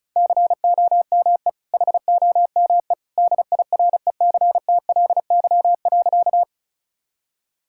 Morse code, heard as a keyed tone, about 36 wpm, 700 Hz. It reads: COME HOME DIRECTLY.
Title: Sentences from Top 1000 Words - Review Part 5 - 35wpm